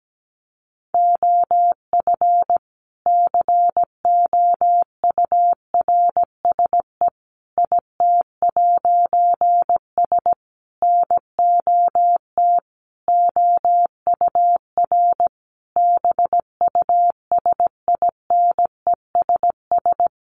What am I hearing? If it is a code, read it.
Morse code, heard as a keyed tone, about 17 wpm, 700 Hz. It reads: OF COURSE IT'S NOT OUR BUSINESS
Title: A Sound Of Thunder 17wpm